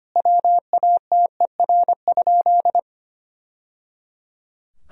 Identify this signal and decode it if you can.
Morse code, heard as a keyed tone, about 25 wpm, 700 Hz. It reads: WATER?
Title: Sentences from Top 1000 Words - Review Part 3 - 25wpm